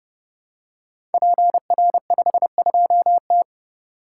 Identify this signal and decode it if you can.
Morse code, heard as a keyed tone, about 30 wpm, 700 Hz. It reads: PR52T
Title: Code Groups - 5 Letter:Numbers - 30wpm